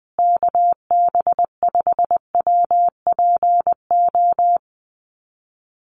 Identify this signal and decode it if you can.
Morse code, heard as a keyed tone, about 20 wpm, 700 Hz. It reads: KB5WPO
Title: USA Call Signs Repeated 2x 20wpm